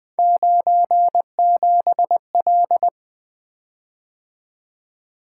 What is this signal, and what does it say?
Morse code, heard as a keyed tone, about 20 wpm, 700 Hz. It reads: 97L
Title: Three Letter-Number Combinations - 20wpm